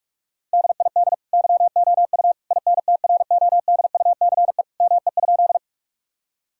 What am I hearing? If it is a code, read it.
Morse code, heard as a keyed tone, about 45 wpm, 700 Hz. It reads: DID YOU INTRODUCE ME?
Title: Sentences from Top 1600 Words 45wpm